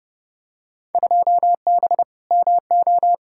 Morse code, heard as a keyed tone, about 30 wpm, 700 Hz. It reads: 2B MO
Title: Field Day Exchanges Encore - 30wpm